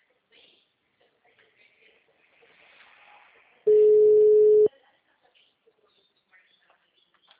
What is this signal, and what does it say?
Ringback tone (Europe, 425 Hz).
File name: out- -2012- - - .51091